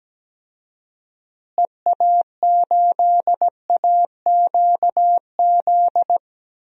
Morse code, heard as a keyed tone, about 17 wpm, 700 Hz. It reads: EA8AQZ
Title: DXCC-100 Call Signs - 6 Characters - Repeated 2x - 17wpm